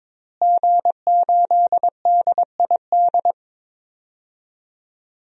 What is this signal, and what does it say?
Morse code, heard as a keyed tone, about 22 wpm, 700 Hz. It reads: G8DID
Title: DXCC-100 Call Signs - 5 Characters - Repeated 3x - 22wpm